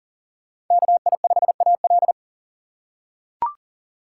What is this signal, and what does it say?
Morse code, heard as a keyed tone, about 40 wpm, 700 Hz. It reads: KI5AL
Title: K1USN SST QSO Practice - 40wpm